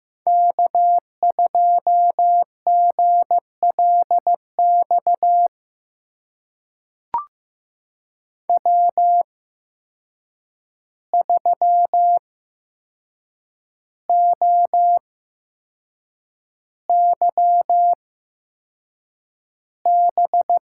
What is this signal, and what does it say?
Morse code, heard as a keyed tone, about 15 wpm, spaced as for 6 wpm, 700 Hz. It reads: K2GLX W3OYB
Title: USA Call Signs - 5 Characters - 8x Character Spacing - 15wpm